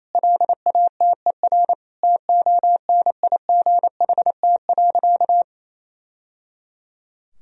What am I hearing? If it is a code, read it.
Morse code, heard as a keyed tone, about 28 wpm, 700 Hz. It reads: LATER TONIGHT.
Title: Sentences from Top 1000 Words - Review Part 4 - 28wpm